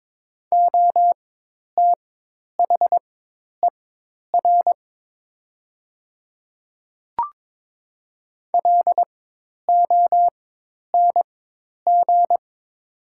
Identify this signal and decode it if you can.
Morse code, heard as a keyed tone, about 22 wpm, spaced as for 10 wpm, 700 Hz. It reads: OTHER LONG
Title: 4x Character Spacing - Top 100 Words - 22wpm